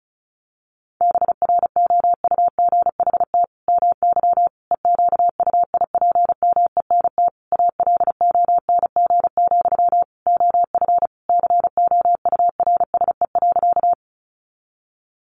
Morse code, heard as a keyed tone, about 34 wpm, 700 Hz. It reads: BROUGHT MY EQUIPMENT ALONG, OF COURSE.
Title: Sentences from Top 700 Words 35wpm